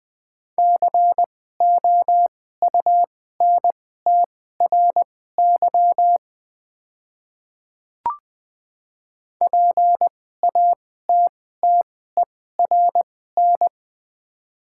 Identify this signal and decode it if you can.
Morse code, heard as a keyed tone, about 20 wpm, spaced as for 14 wpm, 700 Hz. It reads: COUNTRY PATTERN
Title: 2x Character Spacing - Top 500 Words - 20wpm